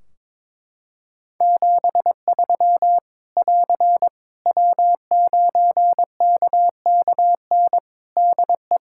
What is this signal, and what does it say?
Morse code, heard as a keyed tone, about 22 wpm, 700 Hz. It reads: 73 <AR> W9KKN DE
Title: Practice QSO - Exchange 4 - Ending - 22wpm